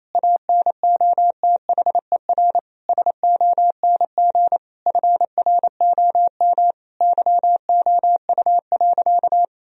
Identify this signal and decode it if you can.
Morse code, heard as a keyed tone, about 28 wpm, 700 Hz. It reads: ANOTHER SONG FROM YOU.
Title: Sentences from Top 1000 Words - Review Part 2 - 28wpm